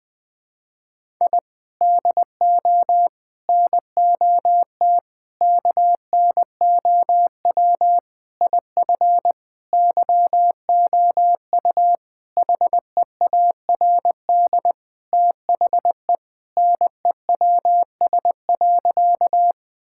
Morse code, heard as a keyed tone, about 20 wpm, 700 Hz. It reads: I DO NOT KNOW IF YOU HEARD THE NEWS.
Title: Sentences with 5-Word Phrases - 20wpm